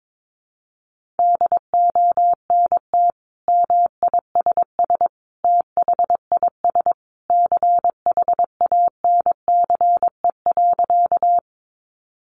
Morse code, heard as a keyed tone, about 22 wpm, 700 Hz. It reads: DONT MISS THIS CHANCE.
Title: Sentences from Top 800 Words 22wpm